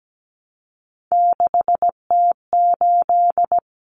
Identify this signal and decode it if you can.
Morse code, heard as a keyed tone, about 17 wpm, 700 Hz. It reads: 6T8
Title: Common Cut Numbers - 3-Digit 17wpm